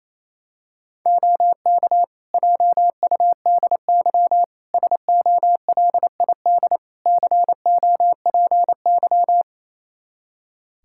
Morse code, heard as a keyed tone, about 28 wpm, 700 Hz. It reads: OK JUDY SOLID COPY
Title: Practice QSO - Exchange 3 - More Info - 28wpm